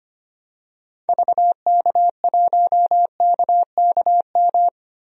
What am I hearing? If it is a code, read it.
Morse code, heard as a keyed tone, about 25 wpm, 700 Hz. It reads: VK1KKM
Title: DXCC-100 Call Signs - 6 Characters - Repeated 3x - 25wpm